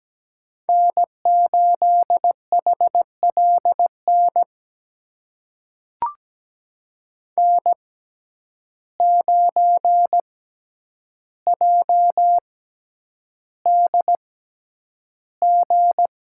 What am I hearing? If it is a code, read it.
Morse code, heard as a keyed tone, about 17 wpm, spaced as for 9 wpm, 700 Hz. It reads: N8HLN N9JDG
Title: USA Call Signs - 5 Characters - 6x Character Spacing - 17wpm